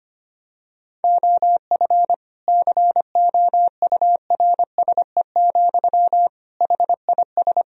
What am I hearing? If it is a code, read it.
Morse code, heard as a keyed tone, about 25 wpm, 700 Hz. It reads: OF COURSE, HIS
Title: Sentences from Top 1000 Words - Review Part 3 - 25wpm